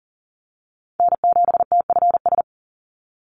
Morse code, heard as a keyed tone, about 40 wpm, 700 Hz. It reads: N7TFS
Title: USA Call Signs 40wpm